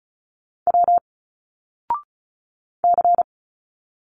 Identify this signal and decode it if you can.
Morse code, heard as a keyed tone, about 34 wpm, 700 Hz. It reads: W C
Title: Single Letters 35wpm